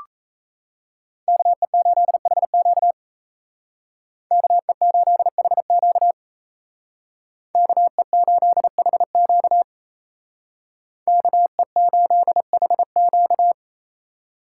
Morse code, heard as a keyed tone, about 35 wpm, 700 Hz. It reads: KE8HQ KE8HQ KE8HQ KE8HQ
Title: Speed Racing - USA Call Signs 28wpm